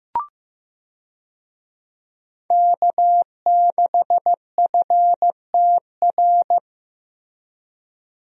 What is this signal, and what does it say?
Morse code, heard as a keyed tone, about 15 wpm, 700 Hz. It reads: K6FTR
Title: USA Call Signs - 5 Characters - Repeated 3x - 15wpm